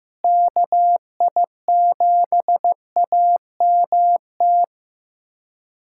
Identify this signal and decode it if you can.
Morse code, heard as a keyed tone, about 15 wpm, 700 Hz. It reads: KI7AMT
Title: USA Callsign DE Callsign - 6 Characters - 15wpm